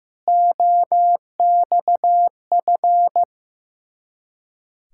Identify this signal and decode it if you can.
Morse code, heard as a keyed tone, about 15 wpm, 690 Hz. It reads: OXF
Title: Ontario QSO Party Exchange 15wpm